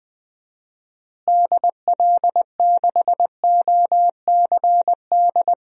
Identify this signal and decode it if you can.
Morse code, heard as a keyed tone, about 20 wpm, 700 Hz. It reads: DL6OCD
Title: DXCC-100 Call Signs - 6 Characters - Repeated 3x - 20wpm